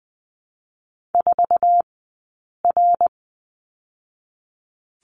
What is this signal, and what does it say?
Morse code, heard as a keyed tone, about 20 wpm, 700 Hz. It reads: 4 R
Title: 066 Lesson 12d - Letter d - Two Characters 20wpm